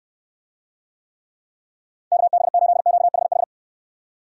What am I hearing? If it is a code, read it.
Morse code, heard as a keyed tone, about 69 wpm, 700 Hz. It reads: KB8QHH